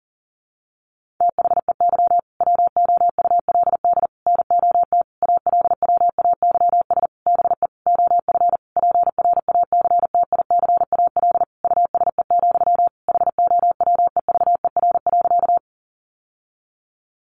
Morse code, heard as a keyed tone, about 40 wpm, 700 Hz. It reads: THEY WOULD NOT ALWAYS BE OF PRACTICAL USE, HOWEVER.
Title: Sentences from Top 1400 Words 40wpm